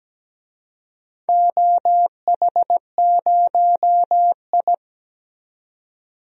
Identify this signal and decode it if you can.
Morse code, heard as a keyed tone, about 17 wpm, 700 Hz. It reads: OH0I
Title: DXCC-100 Call Signs - 4 Characters - 17wpm